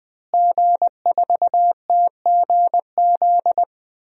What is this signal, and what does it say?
Morse code, heard as a keyed tone, about 20 wpm, 700 Hz. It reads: G4TGZ
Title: DXCC-100 Call Signs - 5 Characters - Repeated 2x - 20wpm